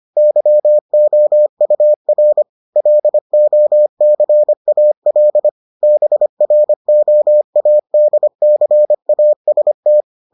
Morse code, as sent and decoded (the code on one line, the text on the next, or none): -.-- --- ..- .-. / .-.. --- -.-. .- .-.. / -... .-. --- .- -.. -.-. .- ... -
YOUR LOCAL BROADCAST